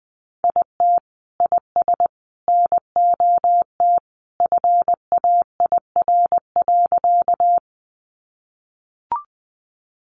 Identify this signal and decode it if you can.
Morse code, heard as a keyed tone, about 20 wpm, 700 Hz. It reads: IT IS NOT FAIR.
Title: Sentences from Top 1300 Words 20wpm